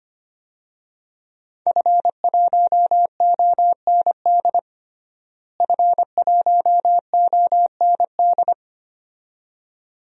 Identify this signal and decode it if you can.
Morse code, heard as a keyed tone, about 25 wpm, 700 Hz. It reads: F1OND F1OND
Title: DXCC-100 Call Signs - 5 Characters - Repeated 2x - 25wpm